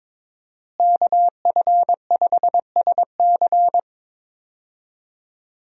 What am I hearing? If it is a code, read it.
Morse code, heard as a keyed tone, about 22 wpm, 700 Hz. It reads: KF5SC
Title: USA Call Signs - 5 Characters - Repeated 2x - 22wpm